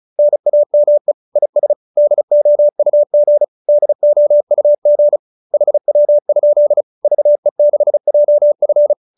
Morse code, heard as a keyed tone, about 34 wpm, 590 Hz. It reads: NAME IS DOUG DOUG HW? VE6JF